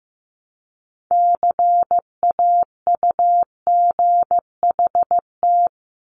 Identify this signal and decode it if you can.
Morse code, heard as a keyed tone, about 15 wpm, 700 Hz. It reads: CAUGHT
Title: Multi-letter Phonemes 15wpm